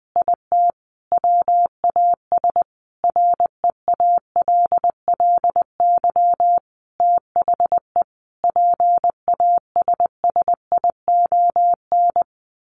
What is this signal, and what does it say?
Morse code, heard as a keyed tone, about 20 wpm, 700 Hz. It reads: IT WAS REALLY THE PASSION
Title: Sentences from Top 1400 Words 20wpm